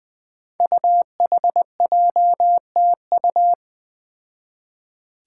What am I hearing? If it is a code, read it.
Morse code, heard as a keyed tone, about 20 wpm, 700 Hz. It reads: UHJTU
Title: Five Challenging Letter-Number Combinations 20wpm